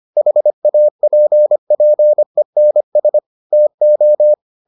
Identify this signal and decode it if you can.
Morse code, heard as a keyed tone, about 25 wpm, 600 Hz. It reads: HAPPENS TO